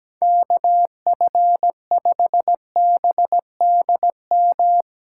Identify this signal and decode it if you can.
Morse code, heard as a keyed tone, about 17 wpm, 700 Hz. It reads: KF5BDM
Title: USA Call Signs - 6 Characters - 6x Character Spacing - 17wpm